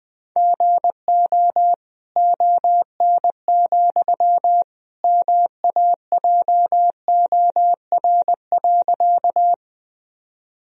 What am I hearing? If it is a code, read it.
Morse code, heard as a keyed tone, about 20 wpm, 700 Hz. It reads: GO ON, MAJOR.
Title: Sentences from Top 1000 Words - Review Part 4 - 20wpm